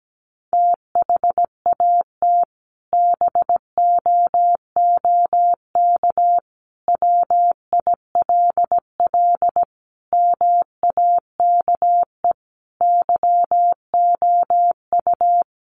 Morse code, heard as a keyed tone, about 17 wpm, 700 Hz. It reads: THAT BOOK WILL MAKE YOU
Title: Sentences from Top 1500 Words 17wpm